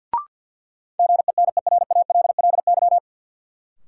Morse code, heard as a keyed tone, about 51 wpm, 700 Hz. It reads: GENERALLY
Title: Sentences from Top 600 Words - New Word Drill - 50wpm